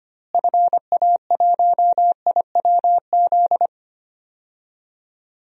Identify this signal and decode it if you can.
Morse code, heard as a keyed tone, about 25 wpm, 700 Hz. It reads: FA1IWZ
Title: DXCC-100 Call Signs - 6 Characters - Repeated 3x - 25wpm